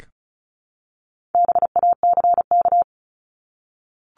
Morse code, heard as a keyed tone, about 34 wpm, 700 Hz. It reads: BACK
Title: Top 300 Words Repeated 35wpm